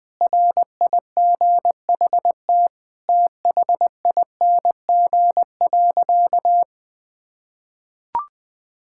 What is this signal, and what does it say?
Morse code, heard as a keyed tone, about 20 wpm, 700 Hz. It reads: RIGHT THING.
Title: Sentences with 5-Word Phrases - 20wpm